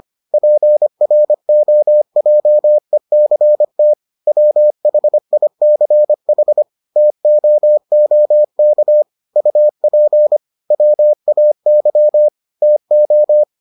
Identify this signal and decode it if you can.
Morse code, heard as a keyed tone, about 25 wpm, 600 Hz. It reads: PROJECT WHICH TOOK UP WAY TO